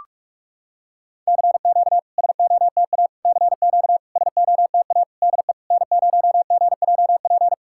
Morse code, heard as a keyed tone, about 46 wpm, 690 Hz. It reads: CQ SOTA CQ SOTA DE N0GJP
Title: SOTA QSO Practice 45wpm